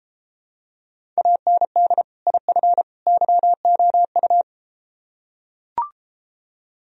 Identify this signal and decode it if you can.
Morse code, heard as a keyed tone, about 33 wpm, 700 Hz. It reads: AND IF YOU